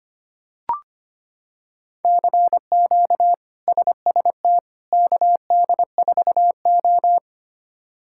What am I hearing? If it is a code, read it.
Morse code, heard as a keyed tone, about 25 wpm, 700 Hz. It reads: CQ SST KD4O
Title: K1USN SST QSO Practice - 25wpm